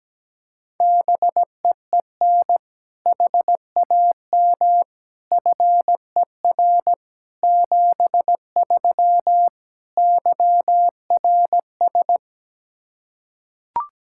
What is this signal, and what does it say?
Morse code, heard as a keyed tone, about 17 wpm, 700 Hz. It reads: BEEN HAM FER 73 YRS
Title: K1USN SST QSO Practice 17wpm